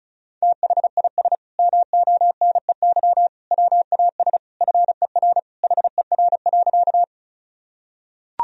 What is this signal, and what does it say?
Morse code, heard as a keyed tone, about 34 wpm, 700 Hz. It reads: THIS MONEY WAS FER HER.
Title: Sentences from Top 400 Words and Abbreviations 35wpm